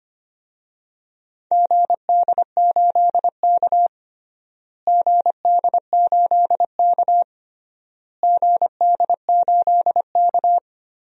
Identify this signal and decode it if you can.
Morse code, heard as a keyed tone, about 25 wpm, 700 Hz. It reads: GD8K GD8K GD8K
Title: DXCC-100 Call Signs - 4 Characters - Repeated 3x - 25wpm